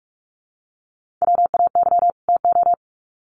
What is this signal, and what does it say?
Morse code, heard as a keyed tone, about 44 wpm, 700 Hz. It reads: WAY TO
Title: Sets of 2 Words 45wpm